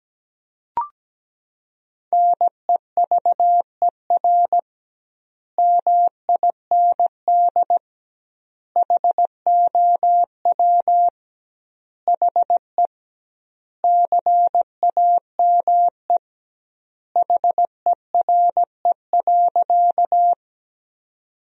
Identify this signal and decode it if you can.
Morse code, heard as a keyed tone, about 17 wpm, 700 Hz. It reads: NEVER MIND HOW HE CAME HERE.
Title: 2x Word-Spacing - Sentences from Top 500 Words - 17wpm